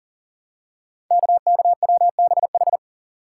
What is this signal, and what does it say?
Morse code, heard as a keyed tone, about 40 wpm, 690 Hz. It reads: KKWBH
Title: Code Groups - 5 Letters - 40wpm